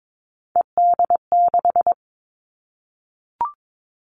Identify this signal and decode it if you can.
Morse code, heard as a keyed tone, about 22 wpm, 700 Hz. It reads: ED6
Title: Callsign Prefixes for Top 100 DXCC - 22wpm